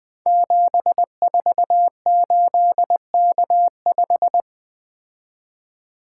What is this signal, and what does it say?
Morse code, heard as a keyed tone, about 20 wpm, 700 Hz. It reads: 748K5
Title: Code Groups - 5 Letter:Numbers - 20wpm